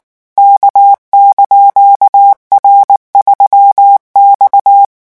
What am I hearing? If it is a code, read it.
Morse code, heard as a keyed tone, about 19 wpm, 800 Hz. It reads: K)R3X
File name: morse mp3 / revision17-vitesse 18 mots